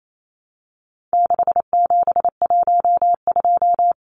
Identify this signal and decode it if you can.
Morse code, heard as a keyed tone, about 28 wpm, 700 Hz. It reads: 6712
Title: Numbers - 4-Digit 28wpm